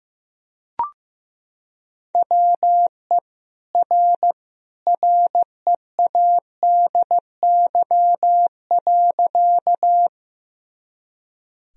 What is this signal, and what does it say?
Morse code, heard as a keyed tone, about 16 wpm, 700 Hz. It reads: WE R READY.
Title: Sentences from Top 400 Words and Abbreviations 15wpm